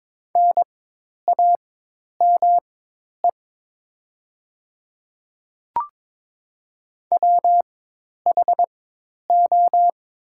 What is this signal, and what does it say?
Morse code, heard as a keyed tone, about 22 wpm, spaced as for 10 wpm, 700 Hz. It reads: NAME WHO
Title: 4x Character Spacing - Top 200 Words - 22wpm